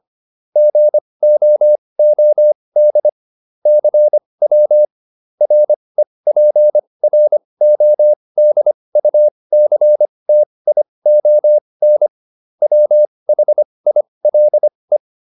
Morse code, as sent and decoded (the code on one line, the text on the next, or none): --. --- --- -.. / -.-. .-- / .-. .--. .-. --- -.. ..- -.-. - .. --- -. / .-- .... .. .-.. .
GOOD CW REPRODUCTION WHILE